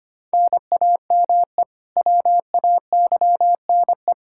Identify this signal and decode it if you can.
Morse code, heard as a keyed tone, about 25 wpm, 700 Hz. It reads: NAME WAYNE
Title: Practice QSO - Exchange 1 - Beginning - 25wpm